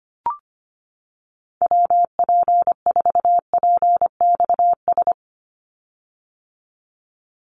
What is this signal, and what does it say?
Morse code, heard as a keyed tone, about 25 wpm, 700 Hz. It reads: WP4PXS
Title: USA Call Signs 25wpm